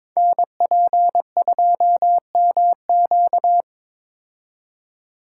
Morse code, heard as a keyed tone, about 21 wpm, 700 Hz. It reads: NP2MQ
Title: Code Groups - 5 Letter:Numbers - 22wpm